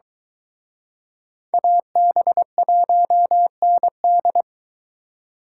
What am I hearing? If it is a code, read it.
Morse code, heard as a keyed tone, about 23 wpm, 700 Hz. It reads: AB1ND